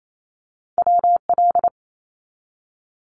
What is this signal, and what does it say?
Morse code, heard as a keyed tone, about 28 wpm, 700 Hz. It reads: WL
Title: Abbreviations 28wpm